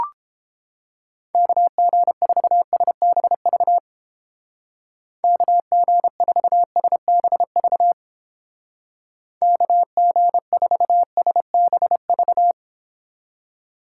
Morse code, heard as a keyed tone, about 29 wpm, 700 Hz. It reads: KG4SBV KG4SBV KG4SBV